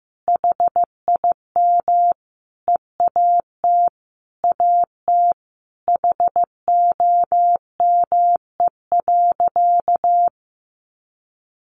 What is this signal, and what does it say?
Morse code, heard as a keyed tone, about 15 wpm, 700 Hz. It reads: HIM EAT AT HOME.
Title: Sentences from Top 1400 Words 15wpm